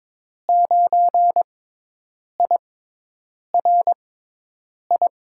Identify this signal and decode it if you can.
Morse code, heard as a keyed tone, about 22 wpm, 700 Hz. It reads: 9 I R I
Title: USA Call Signs - 6x Character Spacing - 22wpm